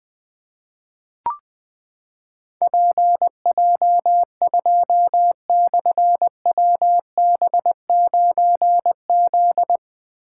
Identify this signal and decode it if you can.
Morse code, heard as a keyed tone, about 21 wpm, 700 Hz. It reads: PJ2/WB9Z
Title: Difficult Call Signs 20wpm